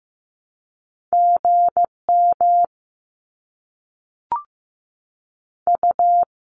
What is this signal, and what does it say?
Morse code, heard as a keyed tone, about 15 wpm, 700 Hz. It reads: GM U
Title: Abbreviations Repeated 3x - 15wpm